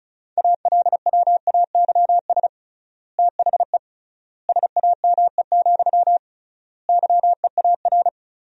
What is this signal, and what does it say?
Morse code, heard as a keyed tone, about 36 wpm, 700 Hz. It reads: ALWAYS THE SAME, YEAR
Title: 3x Word-Spacing - Sentences from Top 300 Words - 35wpm